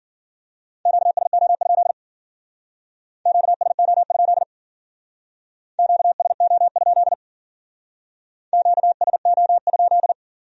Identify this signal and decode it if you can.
Morse code, heard as a keyed tone, about 51 wpm, 700 Hz. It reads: QSO? QSO? QSO? QSO?